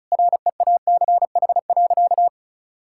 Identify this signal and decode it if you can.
Morse code, heard as a keyed tone, about 36 wpm, 700 Hz. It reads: REACH.
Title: Sentences from Top 1000 Words - Review Part 6 - 35wpm